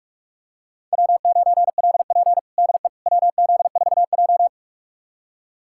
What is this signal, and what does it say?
Morse code, heard as a keyed tone, about 44 wpm, 700 Hz. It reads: W9LP DE WZ4J
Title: USA Callsign DE Callsign - 4 Characters - 45wpm